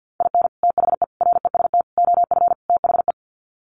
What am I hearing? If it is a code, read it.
Morse code, heard as a keyed tone, about 50 wpm, 700 Hz. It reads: IN THE REST OF THE
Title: Sets of 5 Words 50wpm